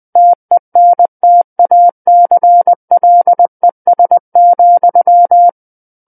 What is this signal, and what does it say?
Morse code, heard as a keyed tone, about 20 wpm, 700 Hz. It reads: TENTACLES,
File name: 20 WPM